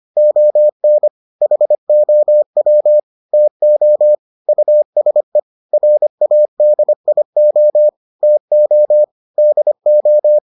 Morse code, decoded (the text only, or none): ON HOW TO USE RADIO TO DO